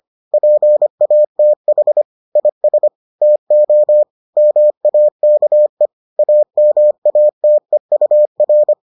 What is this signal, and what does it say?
Morse code, heard as a keyed tone, about 25 wpm, 600 Hz. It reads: PATH IS TO MAKE AMATEUR